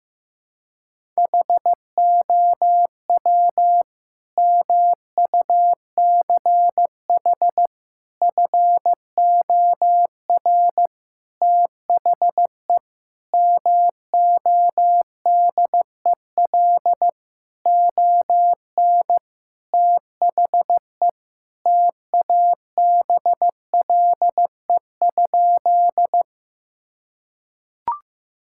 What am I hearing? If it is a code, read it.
Morse code, heard as a keyed tone, about 15 wpm, 700 Hz. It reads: HOW MUCH FOR THE MODEL ON THE TABLE?
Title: Sentences from Top 1000 Words - Review Part 5 - 15wpm